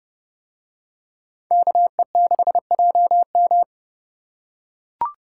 Morse code, heard as a keyed tone, about 28 wpm, 700 Hz. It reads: KE6JM